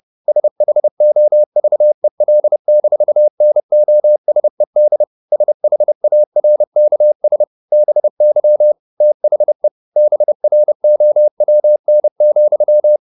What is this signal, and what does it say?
Morse code, heard as a keyed tone, about 30 wpm, 600 Hz. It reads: SHOVEL-NOSED SHARKS BY THE BROWN,